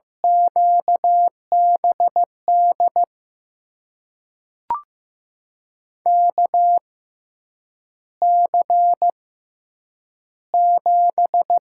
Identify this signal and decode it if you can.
Morse code, heard as a keyed tone, about 16 wpm, spaced as for 8 wpm, 700 Hz. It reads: QBD KC7
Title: USA Call Signs - 6 Characters - 6x Character Spacing - 15wpm